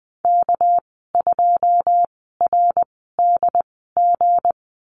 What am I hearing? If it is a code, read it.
Morse code, heard as a keyed tone, about 20 wpm, spaced as for 14 wpm, 700 Hz. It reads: K2RDG
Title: 213 Lesson 34g - Number 8 - Call Signs 20wpm